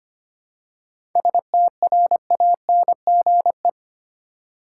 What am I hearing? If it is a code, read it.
Morse code, heard as a keyed tone, about 25 wpm, 700 Hz. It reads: STRANGE
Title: Seven Letter Words - 25wpm